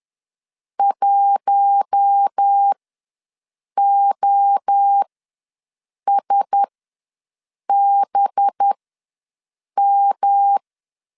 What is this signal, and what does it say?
Morse code, heard as a keyed tone, about 10 wpm, spaced as for 6 wpm, 800 Hz. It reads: 1OSBM
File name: LEZIONI / 5 LBVC / 5 LB ws 10wpm 3616